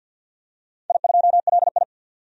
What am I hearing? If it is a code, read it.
Morse code, heard as a keyed tone, about 50 wpm, 700 Hz. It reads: I2LI
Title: DXCC-100 Call Signs - 4 Characters - 50wpm